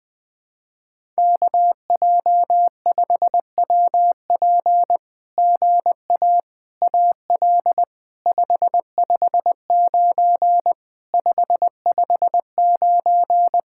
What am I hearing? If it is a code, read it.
Morse code, heard as a keyed tone, about 20 wpm, 700 Hz. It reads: KJ5WP GA AL 559 559
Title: SOTA QSO Practice 20wpm